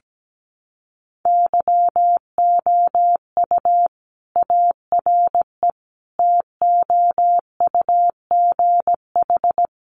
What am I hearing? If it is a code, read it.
Morse code, heard as a keyed tone, about 17 wpm, 700 Hz. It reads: YOU ARE TOUGH
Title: Sentences from Top 1200 Words 17wpm